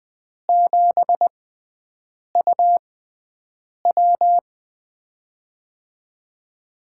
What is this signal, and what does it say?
Morse code, heard as a keyed tone, about 20 wpm, spaced as for 7 wpm, 700 Hz. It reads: 7UW
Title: USA Call Signs - 4 Characters - 6x Character Spacing - 20wpm